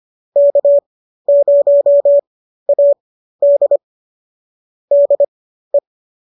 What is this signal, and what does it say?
Morse code, heard as a keyed tone, about 25 wpm, spaced as for 13 wpm, 570 Hz. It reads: K0AD DE